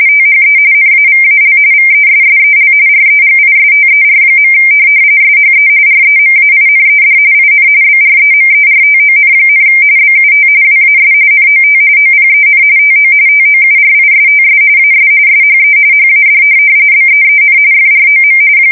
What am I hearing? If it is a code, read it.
RTTY, about 45.45 baud. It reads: THE NAVAL VESSEL REGISTER. SHE REMAINED IN BREMERTON, BUT WAS NOT OPEN TO TOURISTS AS SHE HAD BEEN FROM
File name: fsk170 / USS-Missouri